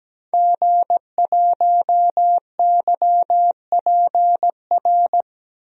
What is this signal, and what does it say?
Morse code, heard as a keyed tone, about 17 wpm, 700 Hz. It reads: G1YPR
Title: DXCC-100 Call Signs - 5 Characters - 17wpm